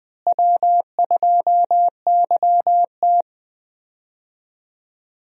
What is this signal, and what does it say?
Morse code, heard as a keyed tone, about 20 wpm, 700 Hz. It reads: W2YT
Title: USA Call Signs - 4 Characters - Repeated 3x - 20wpm